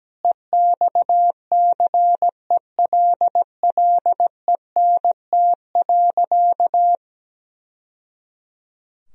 Morse code, heard as a keyed tone, about 17 wpm, 700 Hz. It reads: EXCELLENT.
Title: Sentences from Top 1000 Words - Review Part 4 - 17wpm